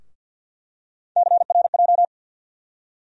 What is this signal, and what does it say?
Morse code, heard as a keyed tone, about 51 wpm, 700 Hz. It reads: CRJ
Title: USA Call Sign Postfixes - 3 Letters 50wpm